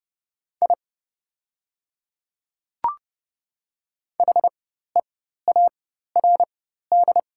Morse code, heard as a keyed tone, about 30 wpm, spaced as for 14 wpm, 700 Hz. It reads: I HEARD
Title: 4x Character Spacing - Top 500 Words - 30wpm